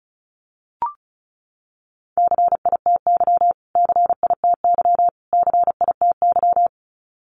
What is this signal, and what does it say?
Morse code, heard as a keyed tone, about 34 wpm, 700 Hz. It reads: CITY CITY CITY
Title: Top 500 Words Repeated 35wpm